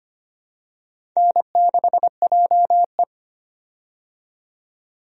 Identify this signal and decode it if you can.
Morse code, heard as a keyed tone, about 25 wpm, 700 Hz. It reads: N6JE
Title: USA Call Signs - 4 Characters - 25wpm